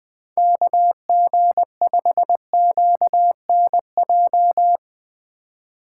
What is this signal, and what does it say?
Morse code, heard as a keyed tone, about 20 wpm, 700 Hz. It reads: KG5QNJ
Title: USA Call Signs - 3x Character Spacing - 20wpm